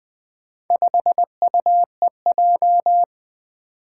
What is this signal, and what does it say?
Morse code, heard as a keyed tone, about 20 wpm, 700 Hz. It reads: 5UEJ
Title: Four Challenging Letter-Number Combinations 20wpm